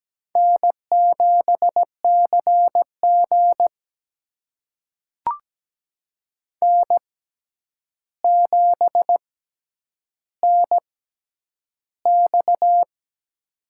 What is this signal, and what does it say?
Morse code, heard as a keyed tone, about 17 wpm, spaced as for 9 wpm, 700 Hz. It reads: N7CG N7NX
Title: USA Call Signs - 6x Character Spacing - 17wpm